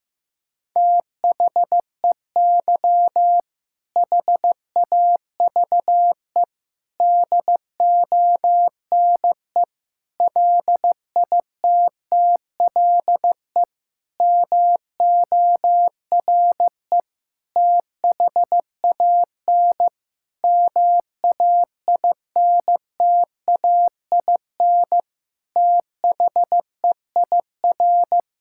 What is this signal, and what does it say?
Morse code, heard as a keyed tone, about 16 wpm, 700 Hz. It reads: THEY HAVE DONE LITTLE MORE THAN MAINTAIN THEIR
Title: Sentences from Top 1000 Words - Review Part 5 - 15wpm